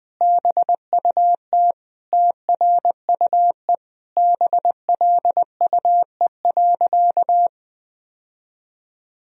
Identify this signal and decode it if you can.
Morse code, heard as a keyed tone, about 21 wpm, 700 Hz. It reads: BUT TRUE BLUE.
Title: Sentences from Top 1000 Words - Review Part 3 - 20wpm